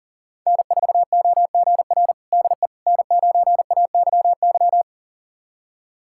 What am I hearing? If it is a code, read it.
Morse code, heard as a keyed tone, about 37 wpm, 700 Hz. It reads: N4OGR DE N9AYY
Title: USA Callsign DE Callsign - 5 Characters - 40wpm